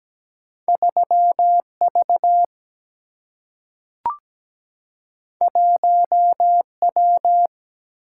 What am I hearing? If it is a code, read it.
Morse code, heard as a keyed tone, about 17 wpm, 700 Hz. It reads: 3V 1W
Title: Single-Letter Single-Number 17wpm